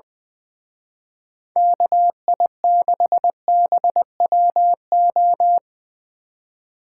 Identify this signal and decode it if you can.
Morse code, heard as a keyed tone, about 24 wpm, 700 Hz. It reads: KI6BWO
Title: Speed Racing - USA 2x3 Call Signs 20wpm